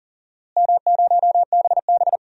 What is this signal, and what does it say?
Morse code, heard as a keyed tone, about 40 wpm, 700 Hz. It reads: M0BB
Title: DXCC-100 Call Signs - 4 Characters - Repeated 3x - 40wpm